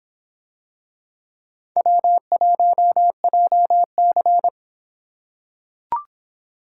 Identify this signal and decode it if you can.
Morse code, heard as a keyed tone, about 26 wpm, 700 Hz. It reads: W1JC